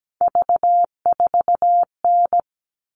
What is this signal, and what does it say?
Morse code, heard as a keyed tone, about 17 wpm, 700 Hz. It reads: V4N
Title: All Cut Numbers - 3-Digit 17wpm